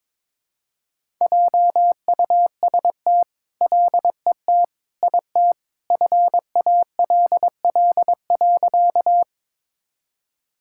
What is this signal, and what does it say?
Morse code, heard as a keyed tone, about 22 wpm, 700 Hz. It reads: JUST LET IT FALL.
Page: morse-code-ninja-practice-22wpm